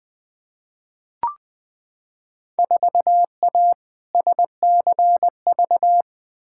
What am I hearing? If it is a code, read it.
Morse code, heard as a keyed tone, about 20 wpm, 700 Hz. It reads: 4A SCV
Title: Field Day Exchanges - 20wpm